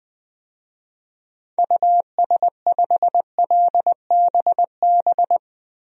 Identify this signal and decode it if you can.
Morse code, heard as a keyed tone, about 20 wpm, 700 Hz. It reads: US5LBB
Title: DXCC-100 Call Signs - 6 Characters - Repeated 2x - 20wpm